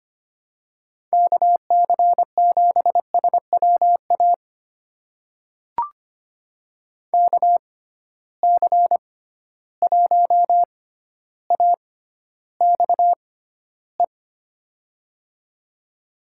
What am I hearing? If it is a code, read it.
Morse code, heard as a keyed tone, about 25 wpm, spaced as for 13 wpm, 700 Hz. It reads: KC7SWA KC1AXE
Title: USA Call Signs - 6 Characters - 6x Character Spacing - 25wpm